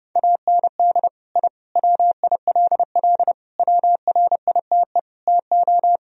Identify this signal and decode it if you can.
Morse code, heard as a keyed tone, about 32 wpm, 700 Hz. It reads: AND I WILL WRITE TO
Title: Sentences from Top 1000 Words - Review Part 1 - 30wpm